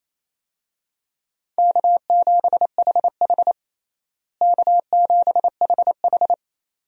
Morse code, heard as a keyed tone, about 28 wpm, 700 Hz. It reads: K7HH K7HH
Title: USA Call Signs - 4 Characters - Repeated 2x - 28wpm